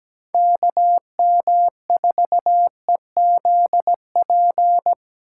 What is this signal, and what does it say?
Morse code, heard as a keyed tone, about 19 wpm, 700 Hz. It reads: KM4EZP
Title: Speed Racing - USA 2x3 Call Signs 17wpm